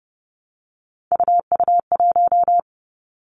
Morse code, heard as a keyed tone, about 30 wpm, 700 Hz. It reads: UU1
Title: Three Challenging Letter-Number Combinations 30wpm